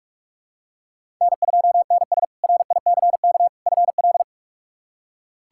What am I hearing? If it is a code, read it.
Morse code, heard as a keyed tone, about 41 wpm, 700 Hz. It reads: N2NS RICK FL
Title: K1USN SST QSO Practice - 45wpm